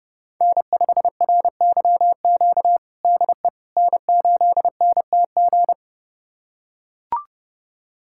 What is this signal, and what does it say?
Morse code, heard as a keyed tone, about 31 wpm, 700 Hz. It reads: N5RYQ DE N8NTG
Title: USA Callsign DE Callsign - 5 Characters - 30wpm